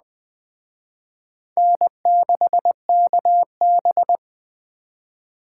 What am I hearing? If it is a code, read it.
Morse code, heard as a keyed tone, about 20 wpm, 700 Hz. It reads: N6KB
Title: Speed Racing - USA Call Signs 17wpm